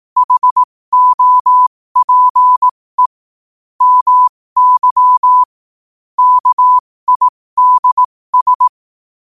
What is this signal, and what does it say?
Morse code, heard as a keyed tone, about 18 wpm, spaced as for 15 wpm, 1000 Hz. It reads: HOPE MY KIDS
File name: qotdfiles / 15wpm